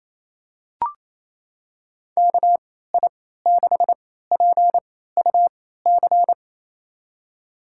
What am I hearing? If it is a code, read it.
Morse code, heard as a keyed tone, about 27 wpm, spaced as for 16 wpm, 700 Hz. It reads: KI6PUC
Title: USA Call Signs - 3x Character Spacing - 28wpm